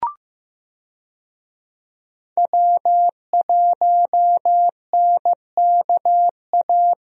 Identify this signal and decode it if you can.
Morse code, heard as a keyed tone, about 15 wpm, 700 Hz. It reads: W1NKA
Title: USA Call Signs - 5 Characters - Repeated 3x - 15wpm